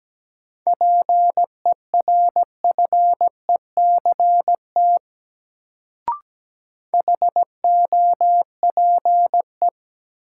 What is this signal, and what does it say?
Morse code, heard as a keyed tone, about 17 wpm, 700 Hz. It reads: PERFECT HOPE
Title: Sentences from Top 1000 Words - New Word Drill - 17wpm